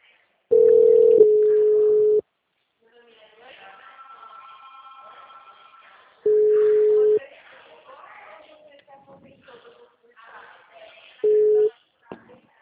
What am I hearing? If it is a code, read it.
Ringback tone (Europe, 425 Hz), with speech.